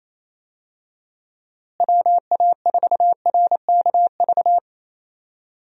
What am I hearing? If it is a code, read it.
Morse code, heard as a keyed tone, about 28 wpm, 700 Hz. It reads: WA4RKV
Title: USA Call Signs - 6 Characters - Repeated 3x - 28wpm